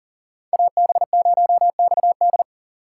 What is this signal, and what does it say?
Morse code, heard as a keyed tone, about 40 wpm, 700 Hz. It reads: AB0XD
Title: USA Call Signs - 5 Characters - 3x Character Spacing - 40wpm